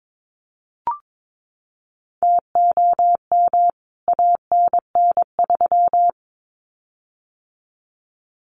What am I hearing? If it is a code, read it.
Morse code, heard as a keyed tone, about 22 wpm, 700 Hz. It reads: TOM ANN3
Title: CWT Exchange 22wpm